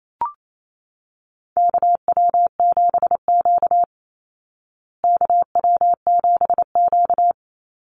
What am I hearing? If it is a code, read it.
Morse code, heard as a keyed tone, about 28 wpm, 700 Hz. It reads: KW7Q KW7Q
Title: Call Signs Repeated 3x - 28wpm